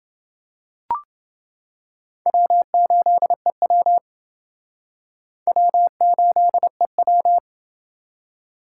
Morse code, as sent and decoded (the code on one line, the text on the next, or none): .-- ---.. .-- / .-- ---.. .--
W8EW W8EW